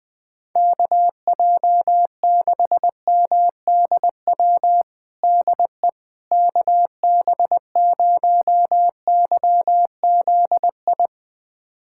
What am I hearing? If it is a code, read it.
Morse code, heard as a keyed tone, about 19 wpm, 700 Hz. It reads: KJ6MDW DE KB0YZI
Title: USA Callsign DE Callsign 20wpm